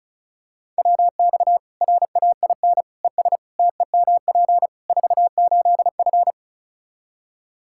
Morse code, heard as a keyed tone, about 34 wpm, 700 Hz. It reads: WX RAIN ES TEMP 48F
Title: Practice QSO - Exchange 2 - Info - 35wpm